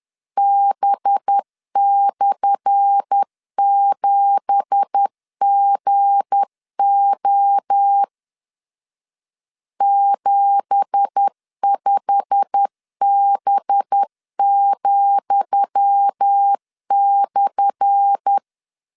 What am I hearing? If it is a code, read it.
Morse code, heard as a keyed tone, about 10 wpm, 800 Hz. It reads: B/7GO 75B,/
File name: LEZIONI / 10 SIGNS / 10D COMA 10wpm 3013